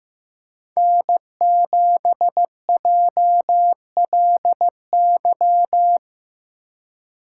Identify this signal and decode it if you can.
Morse code, heard as a keyed tone, about 15 wpm, 700 Hz. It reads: N7JLY
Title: USA Call Signs - 2x Character Spacing - 15wpm